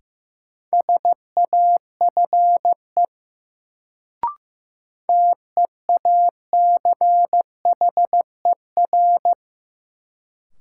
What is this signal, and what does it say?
Morse code, heard as a keyed tone, about 15 wpm, 700 Hz. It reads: SAFE TEACHER
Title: Sentences from Top 900 Words - New Word Drill - 15wpm